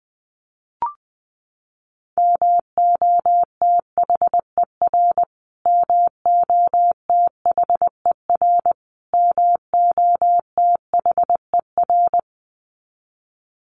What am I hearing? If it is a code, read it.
Morse code, heard as a keyed tone, about 20 wpm, 700 Hz. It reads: MOTHER MOTHER MOTHER
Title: Top 300 Words Repeated 20wpm